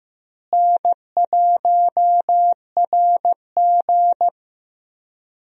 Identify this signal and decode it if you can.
Morse code, heard as a keyed tone, about 15 wpm, 700 Hz. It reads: N1RG
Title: USA Call Signs - 4 Characters - Repeated 2x - 15wpm